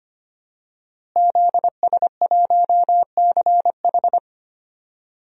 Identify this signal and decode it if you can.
Morse code, heard as a keyed tone, about 25 wpm, 700 Hz. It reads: ZS1CH
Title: DXCC-100 Call Signs - 5 Characters - Repeated 3x - 25wpm